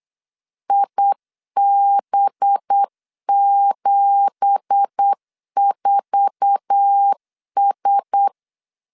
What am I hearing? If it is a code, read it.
Morse code, heard as a keyed tone, about 8 wpm, 800 Hz. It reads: IB74S
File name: LEZIONI / 9 2867 8wpm 3440